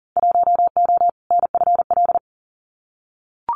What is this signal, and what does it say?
Morse code, heard as a keyed tone, about 40 wpm, 700 Hz. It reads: 1O NFL
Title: Winter Field Day Exchanges 40wpm